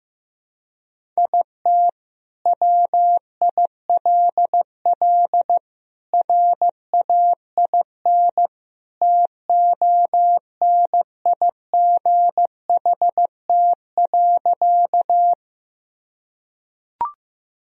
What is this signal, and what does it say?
Morse code, heard as a keyed tone, about 16 wpm, 700 Hz. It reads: IT WILL RAIN TONIGHT.
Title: Sentences from Top 1000 Words - Review Part 1 - 15wpm